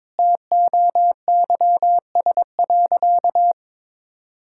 Morse code, heard as a keyed tone, about 22 wpm, 700 Hz. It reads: TOYS.
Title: Pangram Challenge v2 - 22wpm